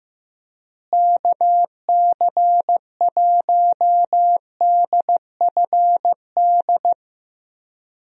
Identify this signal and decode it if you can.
Morse code, heard as a keyed tone, about 15 wpm, 700 Hz. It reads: KC1DFD
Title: USA Call Signs - 6 Characters - Repeated 2x - 15wpm